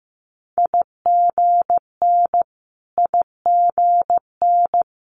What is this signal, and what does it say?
Morse code, heard as a keyed tone, about 15 wpm, 700 Hz. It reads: IGN IGN
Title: Multi-letter Phonemes Repeated 3x 15wpm